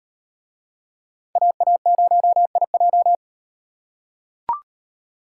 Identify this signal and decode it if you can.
Morse code, heard as a keyed tone, about 38 wpm, 700 Hz. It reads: AA0IJ